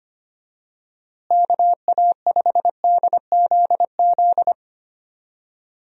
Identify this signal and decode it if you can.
Morse code, heard as a keyed tone, about 25 wpm, 700 Hz. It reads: KA5DZZ
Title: USA Call Signs - 6 Characters - Repeated 2x - 25wpm